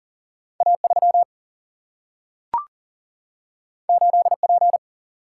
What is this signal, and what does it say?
Morse code, heard as a keyed tone, about 39 wpm, 700 Hz. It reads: A 3 8 P
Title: Single-Letter Single-Number 40wpm